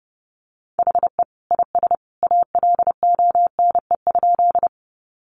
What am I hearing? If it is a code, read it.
Morse code, heard as a keyed tone, about 30 wpm, 700 Hz. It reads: HE IS ALONE?
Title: Sentences from Top 1000 Words 30wpm